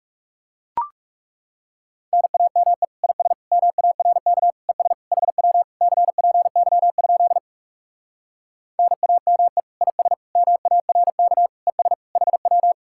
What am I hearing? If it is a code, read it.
Morse code, heard as a keyed tone, about 40 wpm, 700 Hz. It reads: NAME IS MARK ES HW CPY? NAME IS MARK ES HW